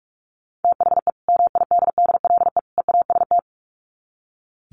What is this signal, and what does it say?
Morse code, heard as a keyed tone, about 44 wpm, 700 Hz. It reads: THE MIDDLE EAST
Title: Sets of 3 Words 45wpm